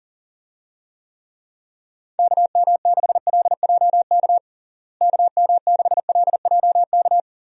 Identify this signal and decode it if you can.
Morse code, heard as a keyed tone, about 40 wpm, 700 Hz. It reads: KM6LJK KM6LJK
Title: USA Call Signs - 6 Characters - Repeated 2x - 40wpm